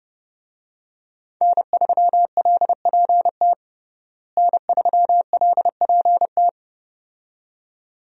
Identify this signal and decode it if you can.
Morse code, heard as a keyed tone, about 30 wpm, 700 Hz. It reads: N3LPT N3LPT
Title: USA Call Signs - 5 Characters - Repeated 2x - 30wpm